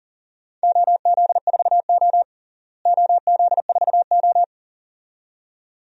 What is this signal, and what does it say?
Morse code, heard as a keyed tone, about 37 wpm, 700 Hz. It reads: OZ4O OZ4O
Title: DXCC-100 Call Signs - 4 Characters - Repeated 2x - 40wpm